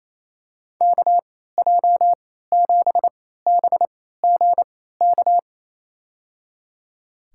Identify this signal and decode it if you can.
Morse code, heard as a keyed tone, about 28 wpm, spaced as for 16 wpm, 700 Hz. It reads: KJ7BGK
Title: USA Call Signs - 6 Characters - 3x Character Spacing - 28wpm